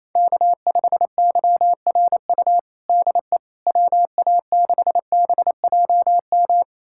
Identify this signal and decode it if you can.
Morse code, heard as a keyed tone, about 28 wpm, 700 Hz. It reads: K5YRU DE WA6BJM
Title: USA Callsign DE Callsign 28wpm